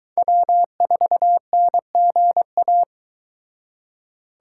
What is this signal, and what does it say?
Morse code, heard as a keyed tone, about 22 wpm, 700 Hz. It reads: W4NGA